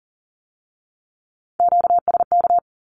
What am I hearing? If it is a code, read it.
Morse code, heard as a keyed tone, about 40 wpm, 700 Hz. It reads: QSK
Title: Q-Codes with Spoken Meaning - 40wpm